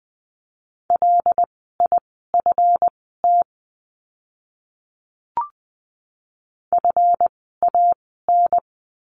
Morse code, heard as a keyed tone, about 20 wpm, spaced as for 14 wpm, 700 Hz. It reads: LIFT FAN
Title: 139 Lesson 23f - Letter f - Words 20wpm